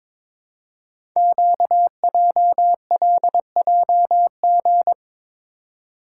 Morse code, heard as a keyed tone, about 22 wpm, 700 Hz. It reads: QJLJG
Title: Code Groups - 5 Letters - 22wpm